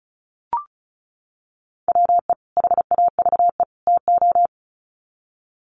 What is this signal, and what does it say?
Morse code, heard as a keyed tone, about 34 wpm, 700 Hz. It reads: WE HAVE TO
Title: Sets of 3 Words 35wpm